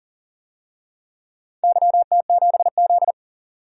Morse code, heard as a keyed tone, about 38 wpm, 700 Hz. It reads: YT7Z
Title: DXCC-100 Call Signs - 4 Characters - 40wpm